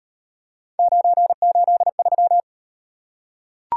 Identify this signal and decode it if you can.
Morse code, heard as a keyed tone, about 38 wpm, 700 Hz. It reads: 983